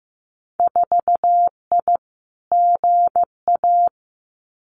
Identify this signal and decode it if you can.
Morse code, heard as a keyed tone, about 15 wpm, 700 Hz. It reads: 4I GA
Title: Winter Field Day Exchanges Encore 15wpm